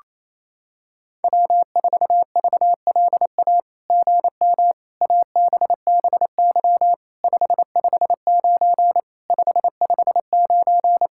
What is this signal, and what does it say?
Morse code, heard as a keyed tone, about 27 wpm, 700 Hz. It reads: W4VLA GM ABBY 559 559
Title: SOTA QSO Practice 28wpm